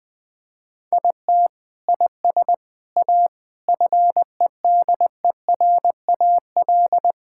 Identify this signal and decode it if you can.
Morse code, heard as a keyed tone, about 20 wpm, 700 Hz. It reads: IT IS A FEDERAL
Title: Sentences from Top 1000 Words - Review Part 5 - 20wpm